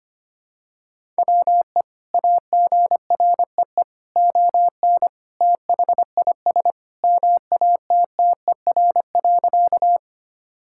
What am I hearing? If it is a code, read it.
Morse code, heard as a keyed tone, about 25 wpm, 700 Hz. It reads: WE AGREE ON THIS MATTER.
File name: practice / 25wpm / Sentences with 2-Word Phrases - 25wpm